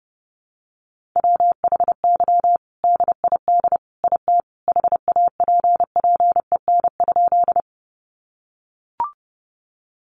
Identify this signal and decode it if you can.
Morse code, heard as a keyed tone, about 30 wpm, 700 Hz. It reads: WHY DID IT HAPPEN?
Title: Sentences from Top 400 Words 30wpm